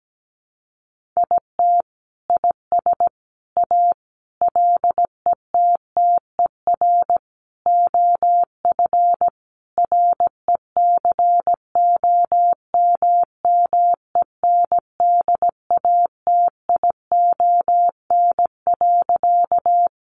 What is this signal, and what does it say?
Morse code, heard as a keyed tone, about 17 wpm, 700 Hz. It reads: IT IS A LETTER OF RECOMMENDATION.
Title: Sentences from Top 1400 Words 17wpm